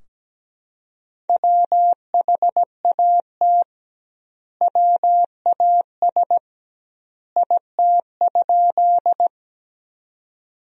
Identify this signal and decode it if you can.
Morse code, heard as a keyed tone, about 17 wpm, 700 Hz. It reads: WHAT WAS IT?
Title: 2x Word-Spacing - Sentences from Top 100 Words - 17wpm